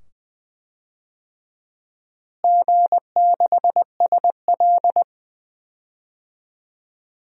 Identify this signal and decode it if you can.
Morse code, heard as a keyed tone, about 20 wpm, 700 Hz. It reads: G6SL
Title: DXCC-100 Call Signs - 4 Characters - Repeated 2x - 20wpm